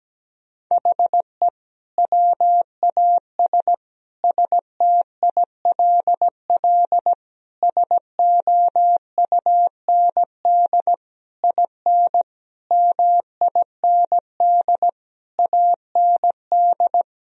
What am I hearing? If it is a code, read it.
Morse code, heard as a keyed tone, about 17 wpm, 700 Hz. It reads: HE WAS STILL SOUND IN MIND AND
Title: Sentences from Top 1000 Words - Review Part 2 - 17wpm